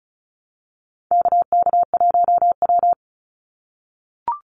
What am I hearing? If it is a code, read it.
Morse code, heard as a keyed tone, about 36 wpm, 700 Hz. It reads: KK1W
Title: NA SOTA Chaser Callsigns 35wpm